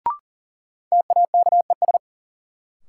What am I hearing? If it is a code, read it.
Morse code, heard as a keyed tone, about 40 wpm, 700 Hz. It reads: TAKES
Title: Sentences from Top 1000 Words - Review Part 6 - New Word Drill - 40wpm